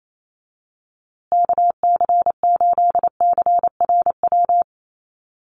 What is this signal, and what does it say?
Morse code, heard as a keyed tone, about 28 wpm, 700 Hz. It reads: KC8CRW
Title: USA Call Signs Repeated 2x 28wpm